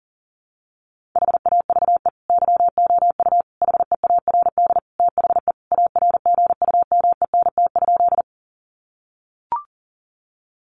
Morse code, heard as a keyed tone, about 40 wpm, 700 Hz. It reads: HAVE YOU HEARD THE ARGUMENT?
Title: Sentences from Top 1400 Words 40wpm